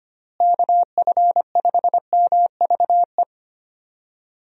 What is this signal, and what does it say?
Morse code, heard as a keyed tone, about 25 wpm, 700 Hz. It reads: KF5MVE
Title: USA Call Signs - 6 Characters - Repeated 3x - 25wpm